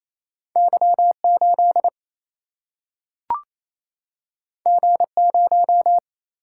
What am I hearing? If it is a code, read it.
Morse code, heard as a keyed tone, about 28 wpm, 700 Hz. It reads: Y8 G0
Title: Single-Letter Single-Number 28wpm